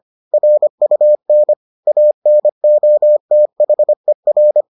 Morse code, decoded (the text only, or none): RUN ANOTHER